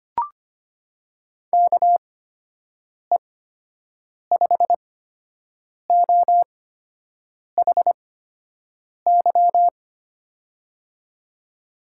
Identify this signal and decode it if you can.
Morse code, heard as a keyed tone, about 25 wpm, spaced as for 7 wpm, 700 Hz. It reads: KE5OHY
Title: USA Call Signs - 6 Characters - 8x Character Spacing - 25wpm